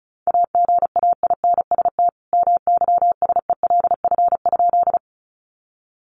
Morse code, heard as a keyed tone, about 34 wpm, 700 Hz. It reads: AGAINST MYSELF?
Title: Sentences from Top 1300 Words 35wpm